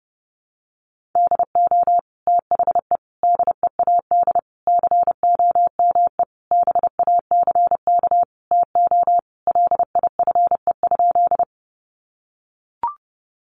Morse code, heard as a keyed tone, about 30 wpm, 700 Hz. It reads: DO THE DEAD COME BACK TO LIFE?
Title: Sentences from Top 1500 Words 30wpm